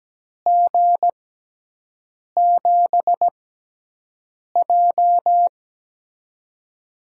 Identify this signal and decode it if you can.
Morse code, heard as a keyed tone, about 17 wpm, spaced as for 6 wpm, 700 Hz. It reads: G7J
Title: USA Call Signs - 4 Characters - 6x Character Spacing - 17wpm